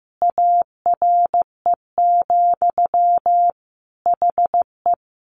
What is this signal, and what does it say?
Morse code, heard as a keyed tone, about 15 wpm, 700 Hz. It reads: ARE, HE
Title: Sentences from Top 100 Words 15wpm